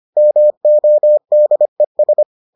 Morse code, heard as a keyed tone, about 25 wpm, 600 Hz. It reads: MODES